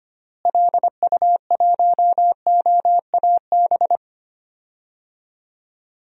Morse code, heard as a keyed tone, about 25 wpm, 700 Hz. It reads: LU1OAB
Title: DXCC-100 Call Signs - 6 Characters - Repeated 3x - 25wpm